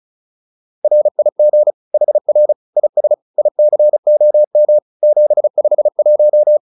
Morse code, heard as a keyed tone, about 34 wpm, 600 Hz. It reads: RIG HR IS ICOM 751